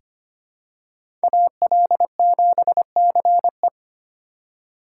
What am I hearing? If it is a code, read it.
Morse code, heard as a keyed tone, about 25 wpm, 700 Hz. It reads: AL7CE
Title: USA Call Signs - 4x Character Spacing - 25wpm